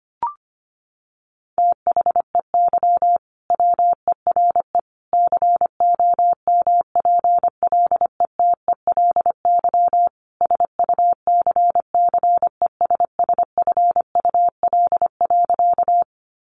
Morse code, heard as a keyed tone, about 25 wpm, 700 Hz. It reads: THEY WERE COMPLETELY SUCCESSFUL.
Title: Sentences from Top 1100 Words 25wpm